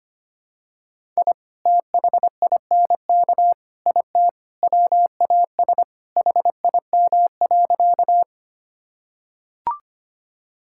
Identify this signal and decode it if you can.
Morse code, heard as a keyed tone, about 25 wpm, 700 Hz. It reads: I THINK IT WAS HIM.
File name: practice / 25wpm / Sentences with 4-Word Phrases - 25wpm